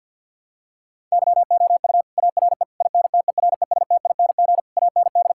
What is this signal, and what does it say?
Morse code, heard as a keyed tone, about 47 wpm, 700 Hz. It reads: YOU ARE INTERESTING AND
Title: Sentences from Top 1000 Words - Review Part 6 - 50wpm